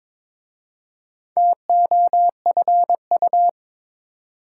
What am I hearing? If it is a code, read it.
Morse code, heard as a keyed tone, about 22 wpm, 700 Hz. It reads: TOFU
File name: Difficult Words with Hints - 22wpm